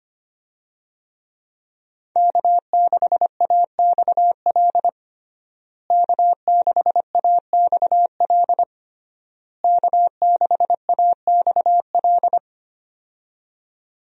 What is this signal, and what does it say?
Morse code, heard as a keyed tone, about 25 wpm, 700 Hz. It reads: K6AXL K6AXL K6AXL
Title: USA Call Signs - 5 Characters - Repeated 3x - 25wpm